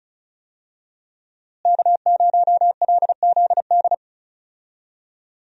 Morse code, heard as a keyed tone, about 34 wpm, 700 Hz. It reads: K0LZD
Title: USA Call Signs - 5 Characters - Repeated 3x - 35wpm